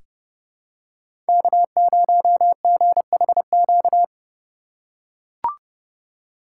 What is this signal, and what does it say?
Morse code, heard as a keyed tone, about 30 wpm, 700 Hz. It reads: K0GHQ
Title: USA Call Signs - 5 Characters - 3x Character Spacing - 30wpm